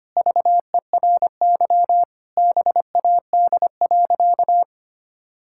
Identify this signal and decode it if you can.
Morse code, heard as a keyed tone, about 25 wpm, 700 Hz. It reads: VERY BAD.
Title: Sentences from Top 1000 Words - Review Part 5 - 25wpm